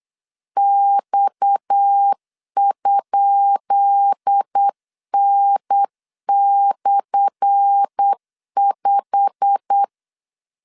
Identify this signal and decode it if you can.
Morse code, heard as a keyed tone, about 8 wpm, 800 Hz. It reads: X?N/5
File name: LEZIONI / 10 SIGNS / 10B INT 8wpm 3534